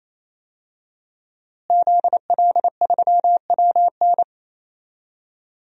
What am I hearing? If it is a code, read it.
Morse code, heard as a keyed tone, about 27 wpm, 700 Hz. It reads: ZL3WN
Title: DXCC-100 Call Signs - 5 Characters - Repeated 3x - 28wpm